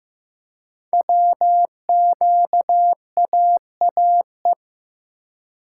Morse code, heard as a keyed tone, about 15 wpm, 700 Hz. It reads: WQAAE
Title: Code Groups - 5 Letter:Numbers - 15wpm